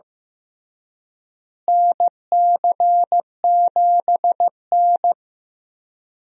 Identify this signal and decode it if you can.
Morse code, heard as a keyed tone, about 16 wpm, 700 Hz. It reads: NC7N
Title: USA Call Signs - 4 Characters - Repeated 3x - 15wpm